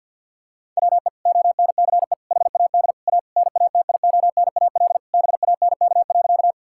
Morse code, heard as a keyed tone, about 50 wpm, 700 Hz. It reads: WE ONCE HAD A NATIONAL BANK.
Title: Sentences from Top 1000 Words - Review Part 6 - 50wpm